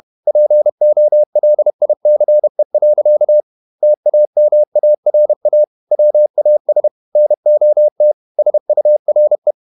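Morse code, heard as a keyed tone, about 31 wpm, 600 Hz. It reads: POLICE. TAMARA WAS NOT SURE
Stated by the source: Morse code at 30 wpm, the novel Never by Ken Follett in English, from CW med Gnister.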